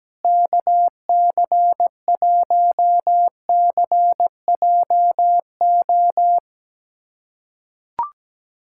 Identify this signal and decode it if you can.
Morse code, heard as a keyed tone, about 17 wpm, 700 Hz. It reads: KC1CJO
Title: USA Callsign DE Callsign 17wpm